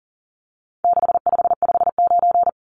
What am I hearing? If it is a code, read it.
Morse code, heard as a keyed tone, about 40 wpm, 700 Hz. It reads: 6559
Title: Numbers - 4-Digit 40wpm